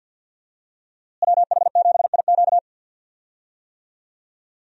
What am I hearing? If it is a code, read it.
Morse code, heard as a keyed tone, about 52 wpm, 700 Hz. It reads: WH7IQ
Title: USA Call Signs - 5 Characters - Repeated 3x - 50wpm